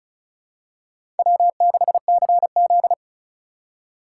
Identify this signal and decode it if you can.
Morse code, heard as a keyed tone, about 34 wpm, 700 Hz. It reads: W6CZ
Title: USA Call Signs - 4 Characters - 8x Character Spacing - 35wpm